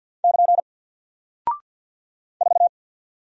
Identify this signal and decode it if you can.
Morse code, heard as a keyed tone, about 49 wpm, 700 Hz. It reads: <KN> 4
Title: Single Character (Letter:Number:Punctuation:Prosign) Alt 50wpm